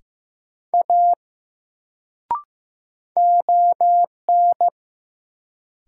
Morse code, heard as a keyed tone, about 15 wpm, 700 Hz. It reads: A ON
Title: Sentences from Top 100 Words - New Word Drill - 15wpm